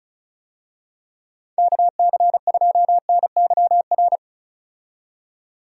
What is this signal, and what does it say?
Morse code, heard as a keyed tone, about 36 wpm, 700 Hz. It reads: KC2NYR
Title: USA Call Signs - 6 Characters - 35wpm